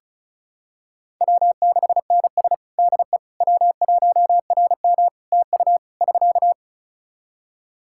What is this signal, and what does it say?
Morse code, heard as a keyed tone, about 36 wpm, 700 Hz. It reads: W6NS DE W1RM TU <SK>
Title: Practice QSO - Exchange 4 - Ending - 35wpm